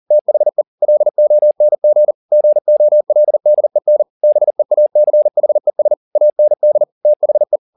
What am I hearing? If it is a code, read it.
Morse code, heard as a keyed tone, about 40 wpm, 600 Hz. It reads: THE LONG GOLDEN BEACHES AND THE